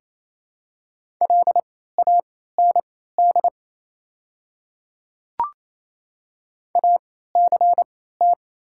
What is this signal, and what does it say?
Morse code, heard as a keyed tone, about 28 wpm, spaced as for 16 wpm, 700 Hz. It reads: LAND ACT
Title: 3x Character Spacing - Top 300 Words - 28wpm